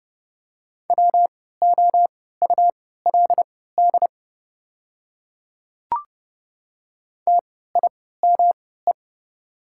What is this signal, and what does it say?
Morse code, heard as a keyed tone, about 31 wpm, spaced as for 17 wpm, 700 Hz. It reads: WOULD TIME
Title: 3x Character Spacing - Top 100 Words - 30wpm